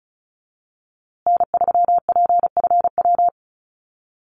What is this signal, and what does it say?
Morse code, heard as a keyed tone, about 34 wpm, 700 Hz. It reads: N3PFW
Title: USA Call Signs Repeated 3x 35wpm